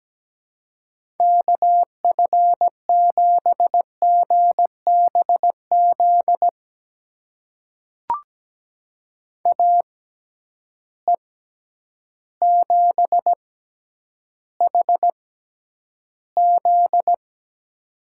Morse code, heard as a keyed tone, about 17 wpm, spaced as for 9 wpm, 700 Hz. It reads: KF7GBZ AE7HZ
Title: USA Call Signs - 6x Character Spacing - 17wpm